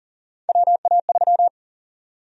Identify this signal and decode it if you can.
Morse code, heard as a keyed tone, about 40 wpm, 700 Hz. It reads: WA3
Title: USA Call Sign Prefixes - 40wpm